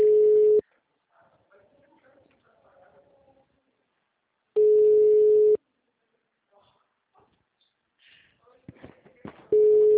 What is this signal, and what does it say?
Ringback tone (Europe, 425 Hz), with speech.